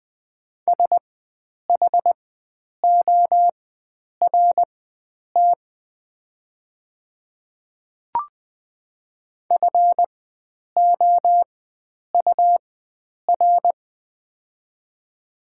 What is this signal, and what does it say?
Morse code, heard as a keyed tone, about 20 wpm, spaced as for 9 wpm, 700 Hz. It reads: SHORT FOUR
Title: 4x Character Spacing - Top 500 Words - 20wpm